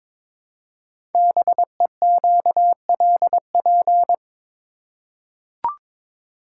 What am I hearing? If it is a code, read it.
Morse code, heard as a keyed tone, about 22 wpm, 700 Hz. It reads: BEQLP
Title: Code Groups - 5 Letters - 22wpm